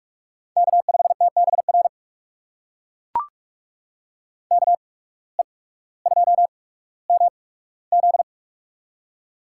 Morse code, heard as a keyed tone, about 42 wpm, spaced as for 19 wpm, 700 Hz. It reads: K5TBR KE2MZ
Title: USA Call Signs - 5 Characters - 8x Character Spacing - 45wpm